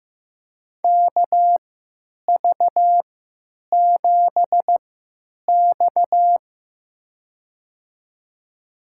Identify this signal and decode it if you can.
Morse code, heard as a keyed tone, about 15 wpm, spaced as for 9 wpm, 700 Hz. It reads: KV7X
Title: USA Call Signs - 4 Characters - 3x Character Spacing - 15wpm